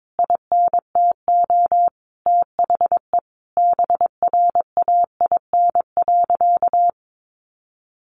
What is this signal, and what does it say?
Morse code, heard as a keyed tone, about 22 wpm, 700 Hz. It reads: INTO THE BRAIN.
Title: A Sound Of Thunder 22wpm